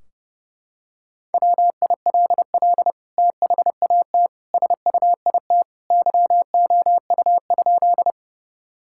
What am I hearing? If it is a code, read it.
Morse code, heard as a keyed tone, about 30 wpm, 700 Hz. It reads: WILL THAT SUIT YOU?
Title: Sentences from Top 1600 Words 30wpm